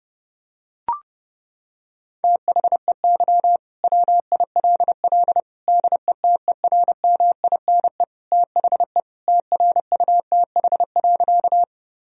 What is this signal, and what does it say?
Morse code, heard as a keyed tone, about 30 wpm, 700 Hz. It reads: THEY WILL DETERMINE THE TRUTH.
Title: Sentences from Top 1000 Words - Review Part 4 - 30wpm